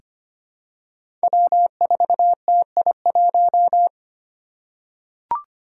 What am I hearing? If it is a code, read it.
Morse code, heard as a keyed tone, about 25 wpm, 700 Hz. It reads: W4TI1
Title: Five Challenging Letter-Number Combinations 25wpm